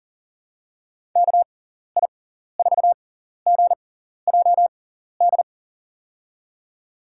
Morse code, heard as a keyed tone, about 40 wpm, spaced as for 14 wpm, 700 Hz. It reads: KI4GJD
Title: USA Call Signs - 6 Characters - 6x Character Spacing - 40wpm